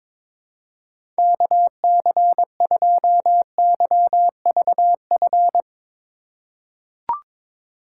Morse code, heard as a keyed tone, about 22 wpm, 700 Hz. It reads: KC2YVF
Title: USA Call Signs - 6 Characters - 2x Character Spacing - 22wpm